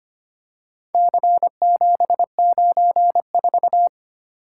Morse code, heard as a keyed tone, about 25 wpm, 700 Hz. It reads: C794
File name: practice / 25wpm / Four Letter-Number Combinations - 25wpm